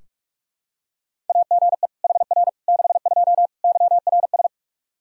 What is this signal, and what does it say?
Morse code, heard as a keyed tone, about 44 wpm, 700 Hz. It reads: AGE HR 62 YRS
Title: Practice QSO - Exchange 3 - More Info - 45wpm